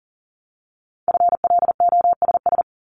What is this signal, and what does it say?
Morse code, heard as a keyed tone, about 39 wpm, 700 Hz. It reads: FLOSS
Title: Multi-letter Phonemes 40wpm